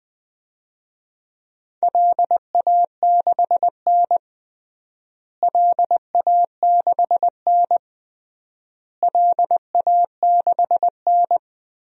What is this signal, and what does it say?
Morse code, heard as a keyed tone, about 20 wpm, 700 Hz. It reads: LA6N LA6N LA6N
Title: DXCC-100 Call Signs - 4 Characters - Repeated 3x - 20wpm